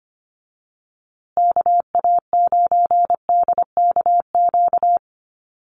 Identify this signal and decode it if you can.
Morse code, heard as a keyed tone, about 25 wpm, 700 Hz. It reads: KA9DKQ
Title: USA Call Signs Repeated 3x 25wpm